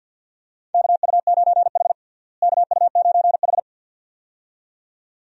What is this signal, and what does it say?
Morse code, heard as a keyed tone, about 52 wpm, 700 Hz. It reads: KU9H KU9H
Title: USA Call Signs - 4 Characters - Repeated 2x - 50wpm